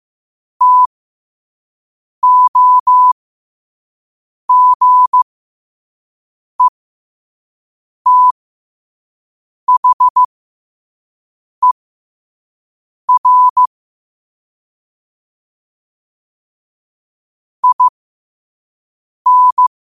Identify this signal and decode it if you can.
Morse code, heard as a keyed tone, about 15 wpm, spaced as for 5 wpm, 1000 Hz. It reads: TOGETHER IN
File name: qotdfiles / 5wpm